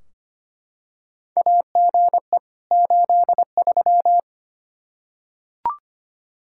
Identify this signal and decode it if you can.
Morse code, heard as a keyed tone, about 25 wpm, 700 Hz. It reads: AGE 83
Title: K1USN SST QSO Practice 25wpm